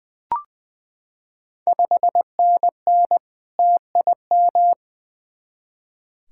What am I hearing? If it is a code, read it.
Morse code, heard as a keyed tone, about 20 wpm, 700 Hz. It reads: 5NN TIM
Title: Ontario QSO Party Exchange 20wpm